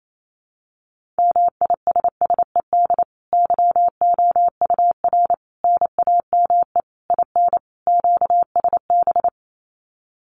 Morse code, heard as a keyed tone, about 28 wpm, 700 Hz. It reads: MISSED YOUR NAME IN QSB
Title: Common QSO Phrases 28wpm